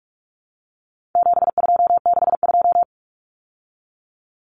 Morse code, heard as a keyed tone, about 44 wpm, 700 Hz. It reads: 7262
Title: Common Cut Numbers - 4-Digit 45wpm